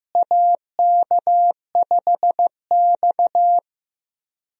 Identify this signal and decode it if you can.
Morse code, heard as a keyed tone, about 15 wpm, 700 Hz. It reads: AK5X
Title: USA Call Signs - 4 Characters - Repeated 2x - 15wpm